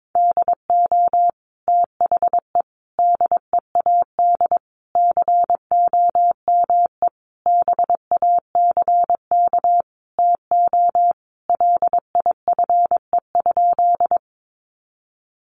Morse code, heard as a keyed tone, about 22 wpm, 700 Hz. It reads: DO THE DEAD COME BACK TO LIFE?
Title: Sentences from Top 1500 Words 22wpm